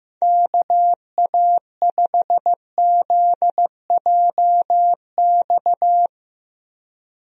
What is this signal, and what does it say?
Morse code, heard as a keyed tone, about 15 wpm, 700 Hz. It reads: KA5ZJX
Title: USA Call Signs - 6 Characters - Repeated 3x - 15wpm